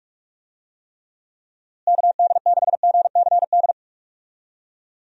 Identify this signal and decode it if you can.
Morse code, heard as a keyed tone, about 46 wpm, 700 Hz. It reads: KD6GCD
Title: USA Call Signs - 6 Characters - Repeated 2x - 45wpm